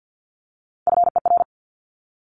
Morse code, heard as a keyed tone, about 50 wpm, 700 Hz. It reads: FER
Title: Abbreviations 50wpm